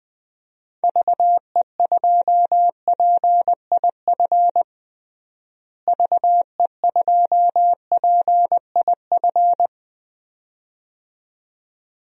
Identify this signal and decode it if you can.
Morse code, heard as a keyed tone, about 20 wpm, 700 Hz. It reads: VE2PIF VE2PIF
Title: DXCC-100 Call Signs - 6 Characters - Repeated 2x - 20wpm